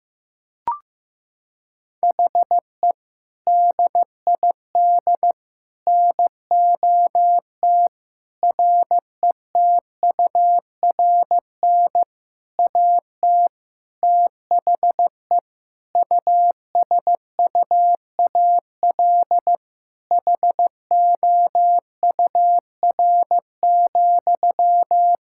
Morse code, heard as a keyed tone, about 15 wpm, 700 Hz. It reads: HE DID NOT RETURN AT THE USUAL HOUR,
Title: Sentences from Top 1600 Words 15wpm